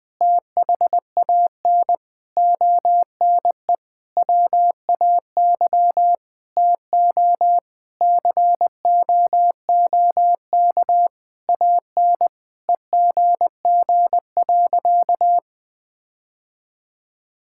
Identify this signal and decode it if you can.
Morse code, heard as a keyed tone, about 20 wpm, 700 Hz. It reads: THAN ONE WAY TO COOK AN EGG.
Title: Sentences with 7-Word Phrases - Spoken Beginning - 20wpm